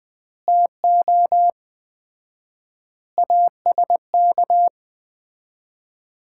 Morse code, heard as a keyed tone, about 20 wpm, 700 Hz. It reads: TO ASK
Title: 4x Word-Spacing - Sets of 5 Words - 20wpm